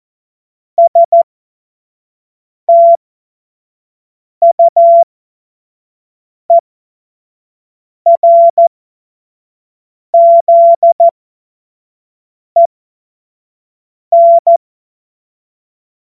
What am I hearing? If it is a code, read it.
Morse code, heard as a keyed tone, about 14 wpm, spaced as for 5 wpm, 680 Hz. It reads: STUERZEN
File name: Morsekurs / Witze / Witz 02 Januar 2024 25 F70